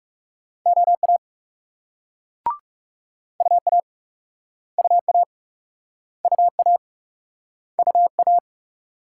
Morse code, heard as a keyed tone, about 38 wpm, 700 Hz. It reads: OA UA UA UA UA